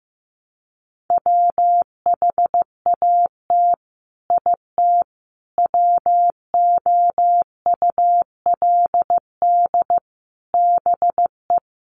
Morse code, heard as a keyed tone, about 15 wpm, 700 Hz. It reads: WHAT IT WOULD BE
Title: Sets of 5 Words Encore 15wpm